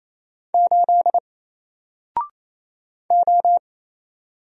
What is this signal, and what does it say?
Morse code, heard as a keyed tone, about 27 wpm, 700 Hz. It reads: 8 O
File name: Single-Letter-Number-28wpm